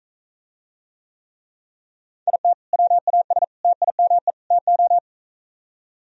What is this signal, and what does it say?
Morse code, heard as a keyed tone, about 42 wpm, 700 Hz. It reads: IT WAS TIME TO